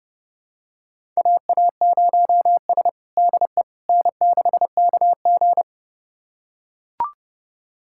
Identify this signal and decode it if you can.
Morse code, heard as a keyed tone, about 30 wpm, 700 Hz. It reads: AA0S DE N6KG
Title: USA Callsign DE Callsign - 4 Characters - 30wpm